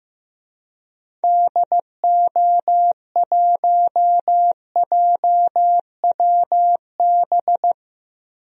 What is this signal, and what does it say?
Morse code, heard as a keyed tone, about 15 wpm, 700 Hz. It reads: DO1JWB
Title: DXCC-100 Call Signs - 6 Characters - 15wpm